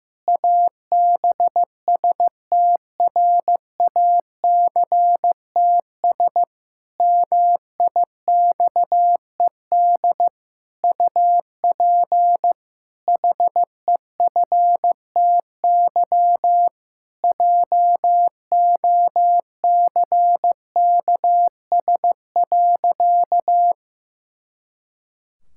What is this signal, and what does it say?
Morse code, heard as a keyed tone, about 15 wpm, 700 Hz. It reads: ABSTRACTS MIXED UP HEFTY JOCKS.
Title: Pangram Challenge v2 - 15wpm